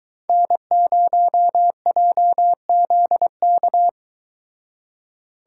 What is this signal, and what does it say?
Morse code, heard as a keyed tone, about 24 wpm, 700 Hz. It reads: N0JZK